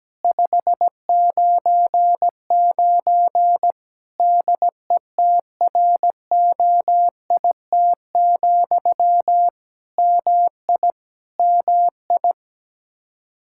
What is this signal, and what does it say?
Morse code, heard as a keyed tone, about 17 wpm, 700 Hz. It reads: 599 DETROIT, MI MI
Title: Practice QSO - Exchange 1 - Beginning - 17wpm